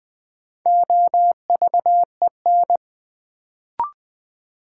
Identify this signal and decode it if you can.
Morse code, heard as a keyed tone, about 20 wpm, 700 Hz. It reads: OVEN
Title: Sentences from Top 1000 Words - Review Part 4 - New Word Drill - 20wpm